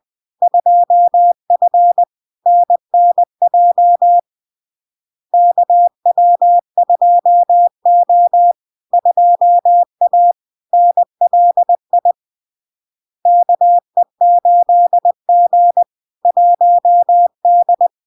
Morse code, heard as a keyed tone, about 20 wpm, 700 Hz. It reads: 2F NNJ KW2O 2A NLI KE8G 1D
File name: LARK-FD-2022-Log-in-CW-20wpm-700-Hz